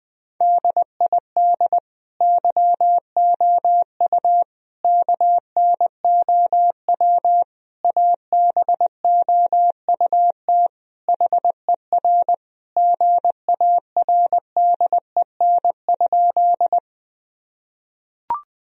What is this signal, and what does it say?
Morse code, heard as a keyed tone, about 21 wpm, 700 Hz. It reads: DID YOU KNOW ABOUT HER GARDEN?
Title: Sentences from Top 1000 Words - Review Part 4 - 20wpm